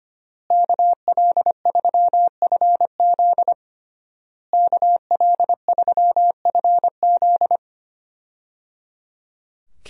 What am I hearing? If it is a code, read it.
Morse code, heard as a keyed tone, about 25 wpm, 700 Hz. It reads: KL3FZ KL3FZ
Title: USA Call Signs - 5 Characters - Repeated 3x - 25wpm